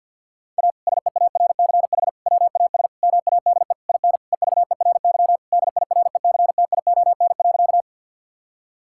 Morse code, heard as a keyed tone, about 50 wpm, 700 Hz. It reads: A SEARCH WAS MADE IN EVERY DIRECTION.
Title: Sentences from Top 1000 Words - Review Part 5 - 50wpm